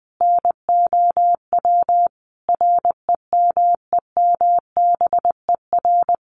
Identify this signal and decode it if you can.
Morse code, heard as a keyed tone, about 20 wpm, 700 Hz. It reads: NOW REMEMBER
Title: Sentences from Top 1200 Words 20wpm